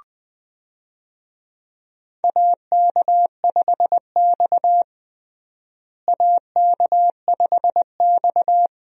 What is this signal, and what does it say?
Morse code, heard as a keyed tone, about 20 wpm, 700 Hz. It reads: AK5X AK5X
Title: USA Call Signs - 4 Characters - Repeated 2x - 20wpm